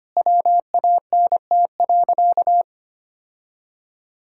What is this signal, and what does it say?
Morse code, heard as a keyed tone, about 25 wpm, 700 Hz. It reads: WANT.
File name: practice / 25wpm / Sentences with 3-Word Phrases - 25wpm